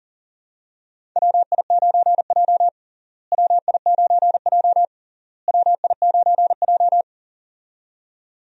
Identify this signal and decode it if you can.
Morse code, heard as a keyed tone, about 37 wpm, 700 Hz. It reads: WI9J WI9J WI9J
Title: USA Call Signs - 4 Characters - Repeated 3x - 40wpm